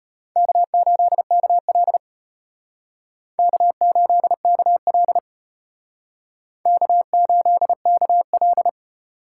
Morse code, heard as a keyed tone, about 32 wpm, 700 Hz. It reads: K8KL K8KL K8KL